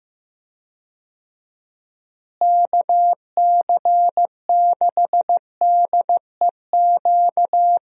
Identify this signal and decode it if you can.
Morse code, heard as a keyed tone, about 15 wpm, 700 Hz. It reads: KC6DEQ
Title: USA Call Signs - 6 Characters - Repeated 2x - 15wpm